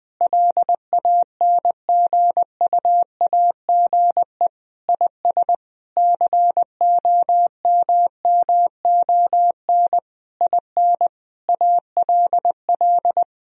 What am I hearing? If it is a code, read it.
Morse code, heard as a keyed tone, about 20 wpm, 700 Hz. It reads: LANGUAGE IS COMMON IN ALL
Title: Sentences from Top 1000 Words - Review Part 3 - 20wpm